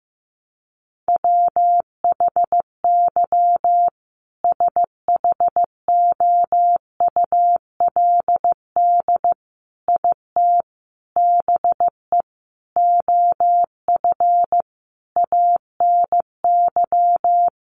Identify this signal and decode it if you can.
Morse code, heard as a keyed tone, about 15 wpm, 700 Hz. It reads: WHY SHOULD IT BE OF ANY
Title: Sentences from Top 1200 Words 15wpm